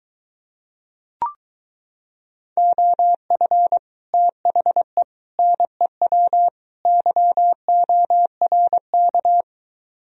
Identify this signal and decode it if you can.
Morse code, heard as a keyed tone, about 24 wpm, 700 Hz. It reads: OF THE NEW YORK